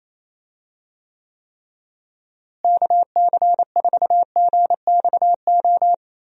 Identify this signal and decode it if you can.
Morse code, heard as a keyed tone, about 27 wpm, 700 Hz. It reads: KC4GXO
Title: USA Call Signs - 6 Characters - Repeated 3x - 28wpm